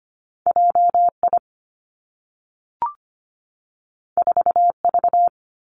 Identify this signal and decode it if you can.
Morse code, heard as a keyed tone, about 25 wpm, 700 Hz. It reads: J I 4 V
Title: Two Challenging Letter-Number Combinations 25wpm